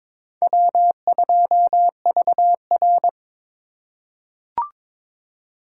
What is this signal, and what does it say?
Morse code, heard as a keyed tone, about 22 wpm, 700 Hz. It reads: W2VR
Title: USA Call Signs - 4 Characters - 4x Character Spacing - 22wpm